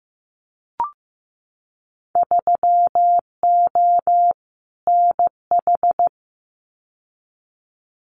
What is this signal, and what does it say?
Morse code, heard as a keyed tone, about 15 wpm, 700 Hz. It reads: 3O NH
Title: Winter Field Day Exchanges Encore 15wpm